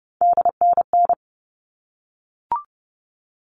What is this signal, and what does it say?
Morse code, heard as a keyed tone, about 30 wpm, 700 Hz. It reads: DNN
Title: All Cut Numbers - 3-Digit 30wpm